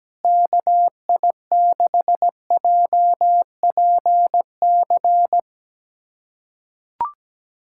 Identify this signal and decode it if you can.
Morse code, heard as a keyed tone, about 17 wpm, 700 Hz. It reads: KI6JPC
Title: USA Callsign DE Callsign 17wpm